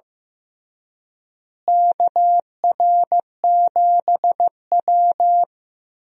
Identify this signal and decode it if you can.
Morse code, heard as a keyed tone, about 15 wpm, 700 Hz. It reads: KR7W
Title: USA Call Signs - 4 Characters - Repeated 3x - 15wpm